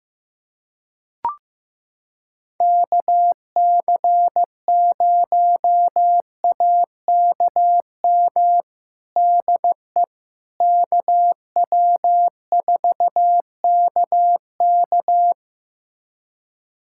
Morse code, heard as a keyed tone, about 15 wpm, 700 Hz. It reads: KC0AKM DE KW4KK
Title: USA Callsign DE Callsign 15wpm